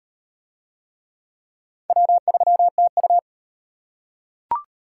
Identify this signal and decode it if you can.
Morse code, heard as a keyed tone, about 33 wpm, 700 Hz. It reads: W3TU